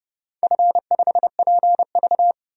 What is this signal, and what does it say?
Morse code, heard as a keyed tone, about 30 wpm, 700 Hz. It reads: F5PV
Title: DXCC-100 Call Signs - 4 Characters - 30wpm